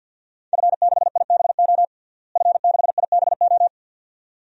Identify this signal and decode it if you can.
Morse code, heard as a keyed tone, about 49 wpm, 700 Hz. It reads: F6IBO F6IBO
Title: DXCC-100 Call Signs - 5 Characters - Repeated 2x - 50wpm